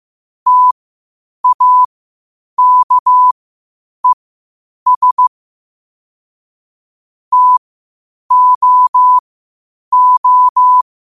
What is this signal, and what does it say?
Morse code, heard as a keyed tone, about 15 wpm, spaced as for 8 wpm, 1000 Hz. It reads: TAKES TOO
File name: qotdfiles / 8wpm